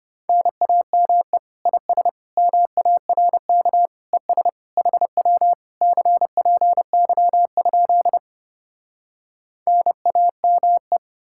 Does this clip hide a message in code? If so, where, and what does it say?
Yes: Morse code from 0.3 to 11.0 s, heard as a keyed tone, about 33 wpm, 700 Hz. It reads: NAME IS MARK ES HW CPY? NAME